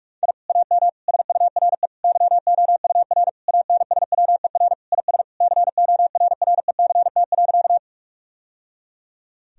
Morse code, heard as a keyed tone, about 46 wpm, 700 Hz. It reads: I AM SURE YOUR ANSWER IS CORRECT.
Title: Sentences from Top 1000 Words - Review Part 2 - 45wpm